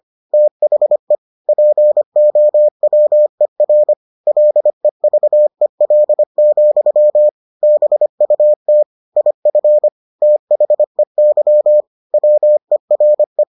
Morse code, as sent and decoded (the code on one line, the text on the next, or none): - .... / .--. --- .-- . .-. / .-.. ...- . .-.. --..-- / -... ..- - / .. ..-. / - .... -.-- / .-- . .-. .
THE POWER LEVEL, BUT IF THEY WERE